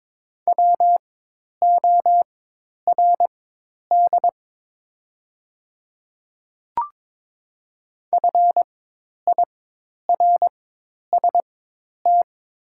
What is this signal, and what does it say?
Morse code, heard as a keyed tone, about 22 wpm, spaced as for 10 wpm, 700 Hz. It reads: WORD FIRST
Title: 4x Character Spacing - Top 100 Words - 22wpm